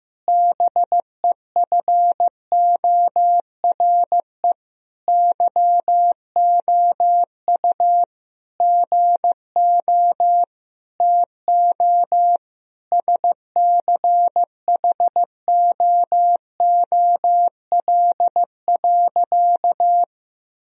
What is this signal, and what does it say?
Morse code, heard as a keyed tone, about 15 wpm, 700 Hz. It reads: BEFORE YOU GO TO SCHOOL.
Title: Sentences from Top 1000 Words - Review Part 2 - 15wpm